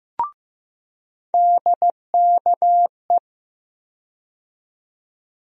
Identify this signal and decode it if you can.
Morse code, heard as a keyed tone, about 15 wpm, 700 Hz. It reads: DKE
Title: USA Call Sign Postfixes - 3 Letters 15wpm